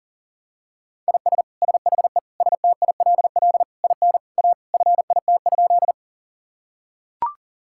Morse code, heard as a keyed tone, about 40 wpm, 700 Hz. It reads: IS SHE STILL IN A FIT?
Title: Sentences from Top 1000 Words - Review Part 5 - 40wpm